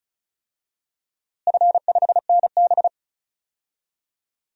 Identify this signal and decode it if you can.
Morse code, heard as a keyed tone, about 36 wpm, 700 Hz. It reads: F5NB
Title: DXCC-100 Call Signs - 4 Characters - Repeated 3x - 35wpm